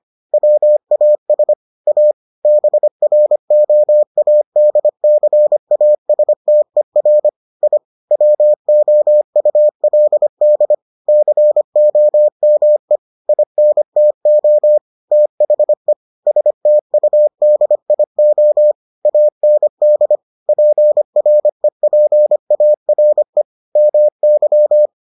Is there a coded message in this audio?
Yes: Morse code from 0.3 to 24.9 s, heard as a keyed tone, about 25 wpm, 600 Hz. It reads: WAS A BROADCASTER I WOULD COME INTO THE STUDIO AND PREPARE MY